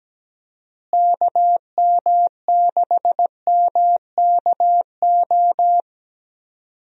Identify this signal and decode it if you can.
Morse code, heard as a keyed tone, about 17 wpm, 700 Hz. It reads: KM6MKO
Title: USA Call Signs - 6 Characters - Repeated 3x - 17wpm